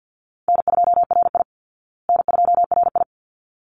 Morse code, heard as a keyed tone, about 50 wpm, 700 Hz. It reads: N2RI N2RI
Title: Call Signs Repeated 3x - 50wpm